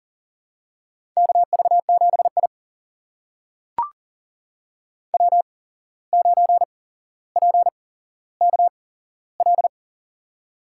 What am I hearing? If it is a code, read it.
Morse code, heard as a keyed tone, about 39 wpm, spaced as for 16 wpm, 700 Hz. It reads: KV7I W9PKL
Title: USA Call Signs - 8x Character Spacing - 40wpm